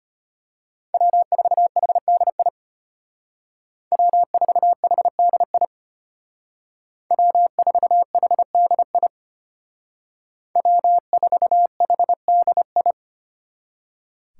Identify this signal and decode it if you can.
Morse code, heard as a keyed tone, about 32 wpm, 700 Hz. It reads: W4HDI W4HDI W4HDI W4HDI